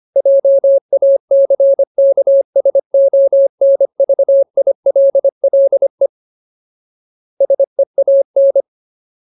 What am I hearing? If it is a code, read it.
Morse code, heard as a keyed tone, about 25 wpm, 550 Hz. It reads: JACKSONVILLE SEAN
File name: Ultimate Word List / 25WPM Ultimate